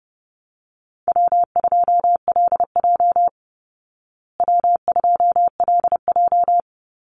Morse code, heard as a keyed tone, about 30 wpm, 700 Hz. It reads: W2LJ W2LJ
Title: Easy Call Signs Repeated 3x - 30wpm